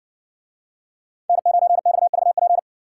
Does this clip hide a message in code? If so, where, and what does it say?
Yes: Morse code from 1.3 to 2.6 s, heard as a keyed tone, about 60 wpm, 700 Hz. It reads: N9XVP